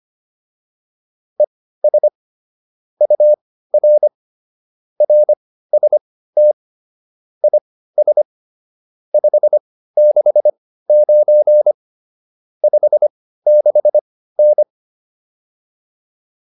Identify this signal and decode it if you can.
Morse code, heard as a keyed tone, about 25 wpm, spaced as for 15 wpm, 600 Hz. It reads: ES UR RST IS 569 56N